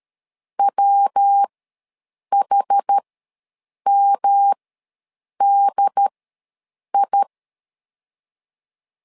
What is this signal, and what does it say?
Morse code, heard as a keyed tone, about 13 wpm, spaced as for 7 wpm, 800 Hz. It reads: WHMDI
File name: LEZIONI / 3 MWRO / 3 MWR ws 12wpm 1809